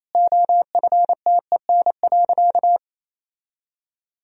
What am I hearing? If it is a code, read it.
Morse code, heard as a keyed tone, about 28 wpm, 700 Hz. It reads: OFTEN.
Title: Sentences from Top 1000 Words - Review Part 3 - 28wpm